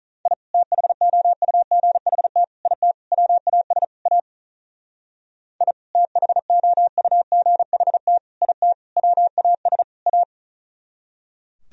Morse code, heard as a keyed tone, about 46 wpm, 700 Hz. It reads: I THOUGHT IT WAS A I THOUGHT IT WAS A